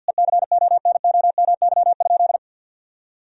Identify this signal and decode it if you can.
Morse code, heard as a keyed tone, about 50 wpm, 700 Hz. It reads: ECONOMY?
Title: Sentences from Top 1000 Words - Review Part 4 - 50wpm